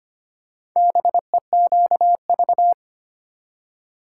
Morse code, heard as a keyed tone, about 25 wpm, 700 Hz. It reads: BEQV
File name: practice / 25wpm / Four Letter-Number Combinations - 25wpm